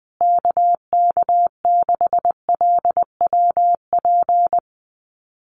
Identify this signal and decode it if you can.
Morse code, heard as a keyed tone, about 20 wpm, 700 Hz. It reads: KK6LWP
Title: USA Call Signs Repeated 2x 20wpm